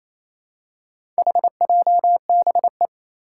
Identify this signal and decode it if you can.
Morse code, heard as a keyed tone, about 28 wpm, 700 Hz. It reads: HJBE
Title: Four Challenging Letter-Number Combinations 28wpm